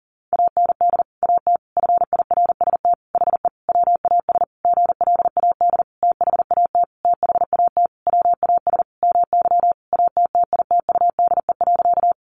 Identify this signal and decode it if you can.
Morse code, heard as a keyed tone, about 40 wpm, 700 Hz. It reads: AND AT FIRST HE WAS GLAD THAT THAT WAS MY ATTITUDE.
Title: Sentences from Top 1300 Words 40wpm